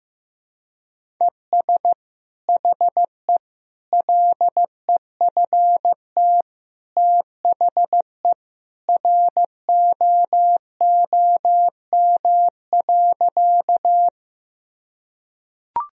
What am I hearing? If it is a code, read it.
Morse code, heard as a keyed tone, about 16 wpm, 700 Hz. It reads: ES HE LEFT THE ROOM.
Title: Sentences from Top 300 Words and Abbreviations 15wpm